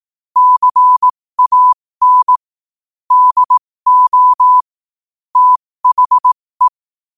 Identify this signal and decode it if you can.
Morse code, heard as a keyed tone, about 18 wpm, spaced as for 15 wpm, 1000 Hz. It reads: CAN DO THE
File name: qotdfiles / 15wpm